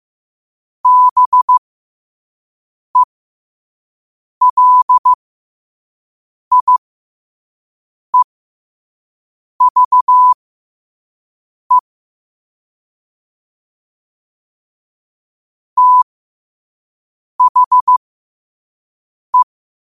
Morse code, heard as a keyed tone, about 15 wpm, spaced as for 5 wpm, 1000 Hz. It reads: BELIEVE THE
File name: qotdfiles / 5wpm